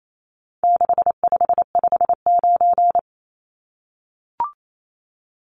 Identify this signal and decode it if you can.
Morse code, heard as a keyed tone, about 28 wpm, 700 Hz. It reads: 6559
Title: Numbers - 4-Digit 28wpm